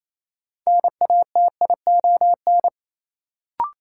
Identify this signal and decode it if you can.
Morse code, heard as a keyed tone, about 28 wpm, 700 Hz. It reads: NATION
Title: Sets of 3 Words - New Word Drill - 28wpm